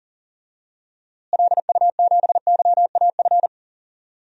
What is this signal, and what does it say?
Morse code, heard as a keyed tone, about 40 wpm, 700 Hz. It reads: LU7YAF
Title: DXCC-100 Call Signs - 6 Characters - Repeated 3x - 40wpm